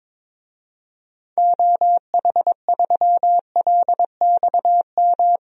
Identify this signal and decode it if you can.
Morse code, heard as a keyed tone, about 22 wpm, 700 Hz. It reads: OH3LXM
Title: DXCC-100 Call Signs - 6 Characters - 22wpm